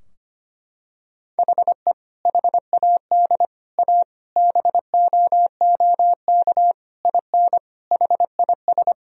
Morse code, heard as a keyed tone, about 25 wpm, 700 Hz. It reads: HE HAD A BOOK IN HIS
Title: Sentences from Top 1000 Words - Review Part 3 - 25wpm